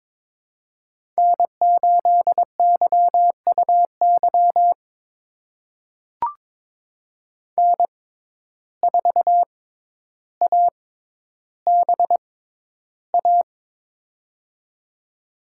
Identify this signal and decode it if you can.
Morse code, heard as a keyed tone, about 22 wpm, spaced as for 11 wpm, 700 Hz. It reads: N8YUY N4ABA
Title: USA Call Signs - 5 Characters - 6x Character Spacing - 22wpm